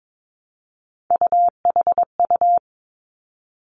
Morse code, heard as a keyed tone, about 22 wpm, 700 Hz. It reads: UHU
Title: Three Challenging Letter-Number Combinations 22wpm